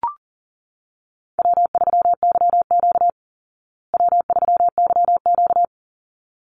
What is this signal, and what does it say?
Morse code, heard as a keyed tone, about 40 wpm, 700 Hz. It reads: W3YQ W3YQ
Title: Easy Call Signs Repeated 2x - 40wpm